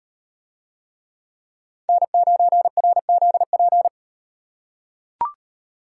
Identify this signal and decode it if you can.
Morse code, heard as a keyed tone, about 38 wpm, 700 Hz. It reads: N9RZP